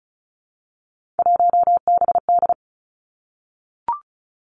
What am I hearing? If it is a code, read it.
Morse code, heard as a keyed tone, about 36 wpm, 700 Hz. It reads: 1BD
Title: Three Challenging Letter-Number Combinations 35wpm